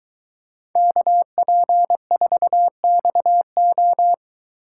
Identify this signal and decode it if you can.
Morse code, heard as a keyed tone, about 22 wpm, 700 Hz. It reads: KP4XO